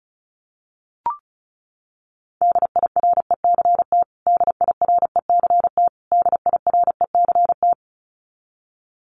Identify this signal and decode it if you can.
Morse code, heard as a keyed tone, about 34 wpm, 700 Hz. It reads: DIRECT DIRECT DIRECT
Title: Top 500 Words Repeated 35wpm